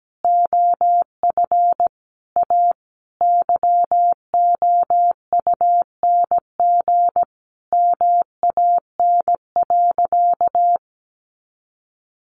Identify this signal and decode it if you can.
Morse code, heard as a keyed tone, about 17 wpm, 700 Hz. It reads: OF A YOUNG MAN.
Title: Sentences from Top 500 Words 17wpm